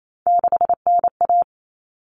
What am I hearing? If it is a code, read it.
Morse code, heard as a keyed tone, about 28 wpm, 700 Hz. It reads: 6NA
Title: Common Cut Numbers - 3-Digit 28wpm